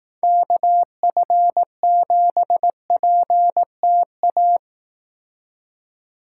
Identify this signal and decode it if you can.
Morse code, heard as a keyed tone, about 18 wpm, 700 Hz. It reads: KF7PTA
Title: Speed Racing - USA 2x3 Call Signs 15wpm